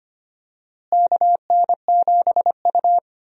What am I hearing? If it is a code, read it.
Morse code, heard as a keyed tone, about 25 wpm, 700 Hz. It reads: KN7U
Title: USA Call Signs - 4 Characters - Repeated 3x - 25wpm